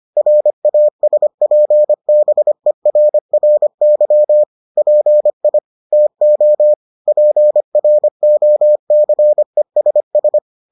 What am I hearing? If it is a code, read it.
Morse code, heard as a keyed tone, about 25 wpm, 600 Hz. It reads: RASPBERRY PI TO PROCESS